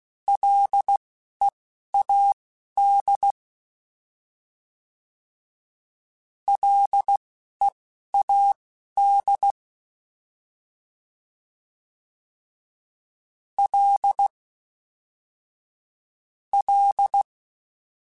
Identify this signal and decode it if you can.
Morse code, heard as a keyed tone, about 16 wpm, spaced as for 11 wpm, 800 Hz. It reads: LEAD LEAD L L